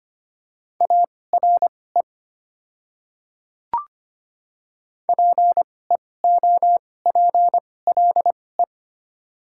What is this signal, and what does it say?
Morse code, heard as a keyed tone, about 25 wpm, spaced as for 18 wpm, 700 Hz. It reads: ARE PEOPLE
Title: 2x Character Spacing - Top 100 Words - 25wpm